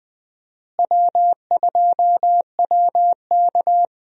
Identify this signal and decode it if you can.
Morse code, heard as a keyed tone, about 21 wpm, 700 Hz. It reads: W2WK
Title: USA Call Signs - 4 Characters - Repeated 3x - 20wpm